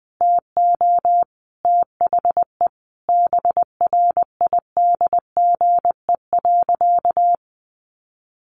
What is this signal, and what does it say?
Morse code, heard as a keyed tone, about 20 wpm, 700 Hz. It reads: TO THE BRIDGE.
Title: Sentences from Top 1200 Words 20wpm